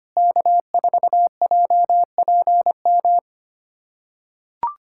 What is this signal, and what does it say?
Morse code, heard as a keyed tone, about 25 wpm, 700 Hz. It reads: K4JPM
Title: USA Callsign DE Callsign - 5 Characters - 25wpm